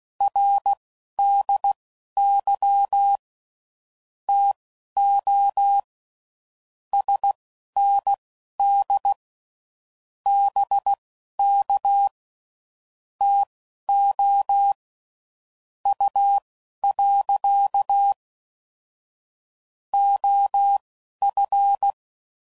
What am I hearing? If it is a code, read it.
Morse code, heard as a keyed tone, about 16 wpm, spaced as for 11 wpm, 800 Hz. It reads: RDY TO SND BK TO U. OF
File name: CW Code Groups @10WPM / QSOs / QSO WA7QOX